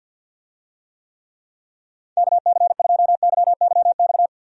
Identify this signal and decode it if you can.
Morse code, heard as a keyed tone, about 50 wpm, 700 Hz. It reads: KC2YYX
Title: USA Call Signs - 6 Characters - Repeated 2x - 50wpm